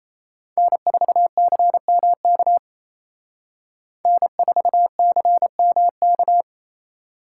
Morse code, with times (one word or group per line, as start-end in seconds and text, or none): 0.6-2.6 N4CMK
4.1-6.4 N4CMK